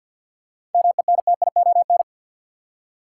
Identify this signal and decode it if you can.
Morse code, heard as a keyed tone, about 52 wpm, 700 Hz. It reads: MENTION
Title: Seven Letter Words - 50wpm